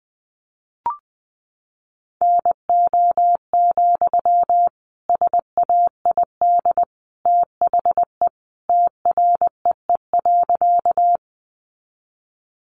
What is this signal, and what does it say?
Morse code, heard as a keyed tone, about 20 wpm, 700 Hz. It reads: NO, SAID THE TREE.
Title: Sentences from Top 300 Words 20wpm